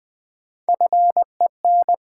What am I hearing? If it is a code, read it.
Morse code, heard as a keyed tone, about 20 wpm, 700 Hz. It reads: FEN
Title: Build A Word - 1 Letter At A Time - 5-Letter Words - 20wpm